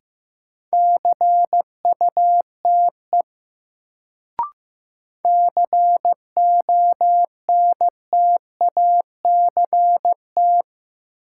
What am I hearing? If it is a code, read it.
Morse code, heard as a keyed tone, about 15 wpm, 700 Hz. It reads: CUTE CONTACT
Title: Sentences from Top 1000 Words - Review Part 7 - New Word Drill - 15wpm